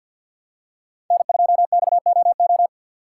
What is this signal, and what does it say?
Morse code, heard as a keyed tone, about 46 wpm, 700 Hz. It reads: N2XOO
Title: USA Call Signs - 5 Characters - 4x Character Spacing - 50wpm